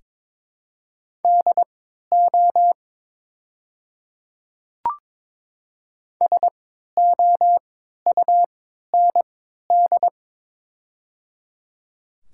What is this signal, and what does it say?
Morse code, heard as a keyed tone, about 22 wpm, spaced as for 13 wpm, 700 Hz. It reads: DO SOUND
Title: 3x Character Spacing - Top 100 Words - 22wpm